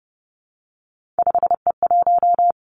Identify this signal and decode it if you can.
Morse code, heard as a keyed tone, about 30 wpm, 700 Hz. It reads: 5E1
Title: Three Challenging Letter-Number Combinations 30wpm